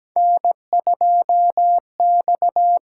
Morse code, heard as a keyed tone, about 17 wpm, 700 Hz. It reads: N2X
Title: USA Call Signs - 5 Characters - 8x Character Spacing - 17wpm